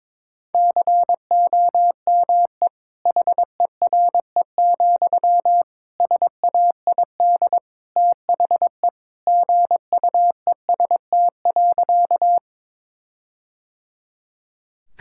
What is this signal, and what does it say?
Morse code, heard as a keyed tone, about 22 wpm, 700 Hz. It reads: COME HERE, SAID THE GUEST.
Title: Sentences from Top 1600 Words 22wpm